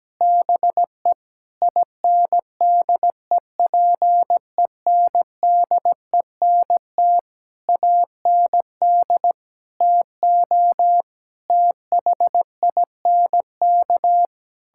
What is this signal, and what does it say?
Morse code, heard as a keyed tone, about 17 wpm, 700 Hz. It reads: BE INDEPENDENT AND TO THINK
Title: Sentences from Top 1000 Words - Review Part 6 - 17wpm